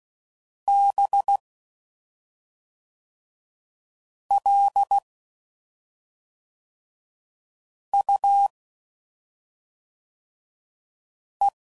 Morse code, heard as a keyed tone, about 16 wpm, 800 Hz. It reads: B L U E
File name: Suffixes / Special ING's / SlowToFaster ING's